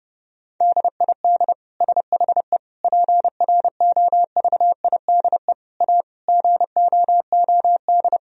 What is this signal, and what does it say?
Morse code, heard as a keyed tone, about 30 wpm, 700 Hz. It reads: DID SHE PROVIDE A GOOD
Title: Sentences from Top 1000 Words - Review Part 4 - 30wpm